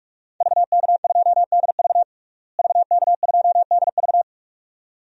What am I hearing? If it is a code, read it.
Morse code, heard as a keyed tone, about 46 wpm, 700 Hz. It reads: VK2DV VK2DV
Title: DXCC-100 Call Signs - 5 Characters - Repeated 2x - 45wpm